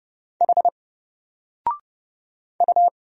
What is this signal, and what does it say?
Morse code, heard as a keyed tone, about 30 wpm, 700 Hz. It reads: H U
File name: Single-Letter-Number-30wpm